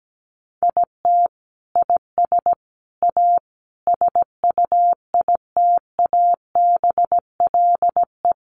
Morse code, heard as a keyed tone, about 17 wpm, 700 Hz. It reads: IT IS A SUITABLE
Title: Sentences from Top 1400 Words 17wpm